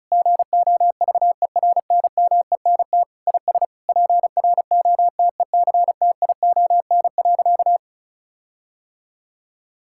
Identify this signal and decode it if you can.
Morse code, heard as a keyed tone, about 36 wpm, 700 Hz. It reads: GOVERNMENT IS PROTECTION.
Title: Sentences from Top 1600 Words 35wpm